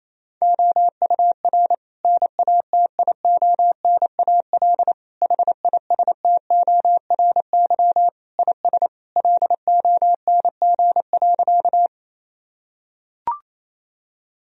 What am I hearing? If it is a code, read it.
Morse code, heard as a keyed tone, about 28 wpm, 700 Hz. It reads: OUR NATIONAL HISTORY IS LONG.
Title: Sentences from Top 1000 Words - Review Part 4 - 28wpm